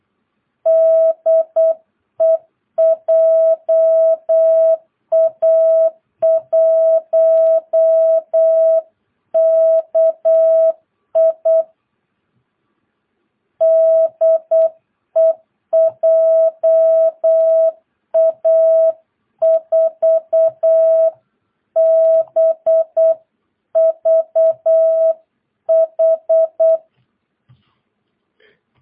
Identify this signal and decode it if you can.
Morse code, heard as a keyed tone, about 8 wpm, 660 Hz. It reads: DE JA1KI DE JA4BVH
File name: jh8cbh / 2022cw